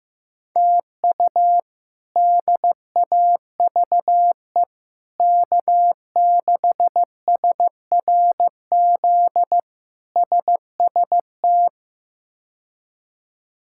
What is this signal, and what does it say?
Morse code, heard as a keyed tone, about 16 wpm, 700 Hz. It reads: TU DAVE K6SRZ SST
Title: K1USN SST QSO Practice - 15wpm